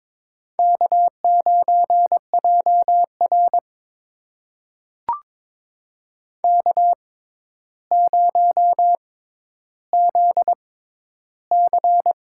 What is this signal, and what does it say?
Morse code, heard as a keyed tone, about 22 wpm, spaced as for 11 wpm, 700 Hz. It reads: K9JR K0ZC
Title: USA Call Signs - 4 Characters - 6x Character Spacing - 22wpm